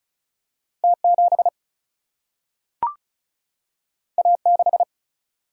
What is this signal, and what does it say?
Morse code, heard as a keyed tone, about 36 wpm, 700 Hz. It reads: T 7 A 6
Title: Single-Letter Single-Number 35wpm